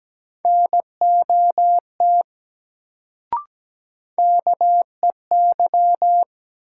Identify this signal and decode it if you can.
Morse code, heard as a keyed tone, about 17 wpm, 700 Hz. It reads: NOT KEY
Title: Sentences from Top 600 Words - New Word Drill - 17wpm